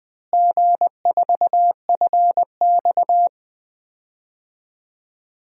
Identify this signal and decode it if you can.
Morse code, heard as a keyed tone, about 20 wpm, 700 Hz. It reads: G4FX
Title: DXCC-100 Call Signs - 4 Characters - Repeated 3x - 20wpm